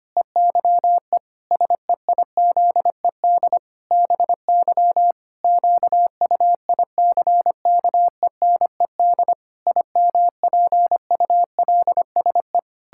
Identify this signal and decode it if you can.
Morse code, heard as a keyed tone, about 25 wpm, 700 Hz. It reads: EYE SEIZED BY QUICKENED IMPULSE
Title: Pangram Challenge v2 - 25wpm